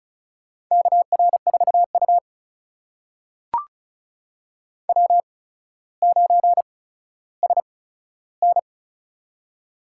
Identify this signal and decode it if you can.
Morse code, heard as a keyed tone, about 36 wpm, spaced as for 15 wpm, 700 Hz. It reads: KR4U W9SN
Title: USA Call Signs - 4 Characters - 8x Character Spacing - 35wpm